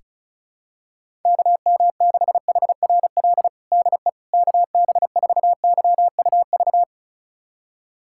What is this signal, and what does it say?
Morse code, heard as a keyed tone, about 34 wpm, 700 Hz. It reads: KM6HRL DE KB4YUV
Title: USA Callsign DE Callsign 35wpm